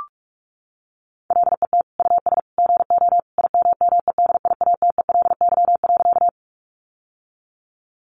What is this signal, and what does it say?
Morse code, heard as a keyed tone, about 44 wpm, 700 Hz. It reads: LET US GO IMMEDIATELY.
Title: Sentences from Top 1100 Words 45wpm